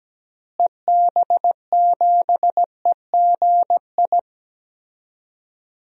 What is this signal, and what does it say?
Morse code, heard as a keyed tone, about 17 wpm, 700 Hz. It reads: EB7EGI
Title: DXCC-100 Call Signs - 6 Characters - Repeated 3x - 17wpm